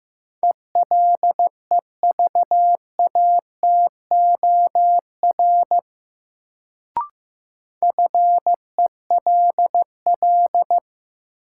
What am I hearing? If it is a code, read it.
Morse code, heard as a keyed tone, about 15 wpm, 700 Hz. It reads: ELEVATOR FELL
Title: Sentences from Top 1600 Words - New Word Drill - 15wpm